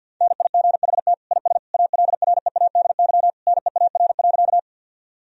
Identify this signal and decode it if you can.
Morse code, heard as a keyed tone, about 52 wpm, 700 Hz. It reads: NIGHT IS ALREADY NEAR.
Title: Sentences from Top 1000 Words - Review Part 1 - 50wpm